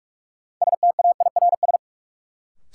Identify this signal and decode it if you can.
Morse code, heard as a keyed tone, about 46 wpm, 700 Hz. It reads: STAIRS
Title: Sets of 5 Words Encore - New Word Drill - 45wpm